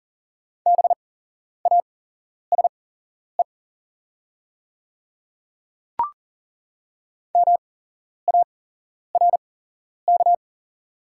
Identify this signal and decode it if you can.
Morse code, heard as a keyed tone, about 38 wpm, spaced as for 11 wpm, 690 Hz. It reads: BASE MARK
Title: 8x Character Spacing - Top 300 Words - 40wpm